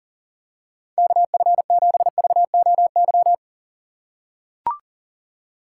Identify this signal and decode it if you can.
Morse code, heard as a keyed tone, about 40 wpm, 700 Hz. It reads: KF7VOY
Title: USA Call Signs - 6 Characters - 3x Character Spacing - 40wpm